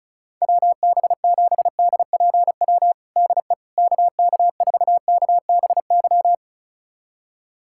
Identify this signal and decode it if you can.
Morse code, heard as a keyed tone, about 36 wpm, 700 Hz. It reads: WB7DPW DE KK4KBY
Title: USA Callsign DE Callsign - 6 Characters - 35wpm